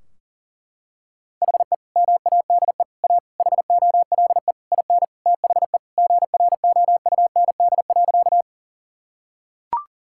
Morse code, heard as a keyed tone, about 40 wpm, 700 Hz. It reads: HE MADE A HOLE IN THE GROUND.
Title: Sentences from Top 1600 Words 40wpm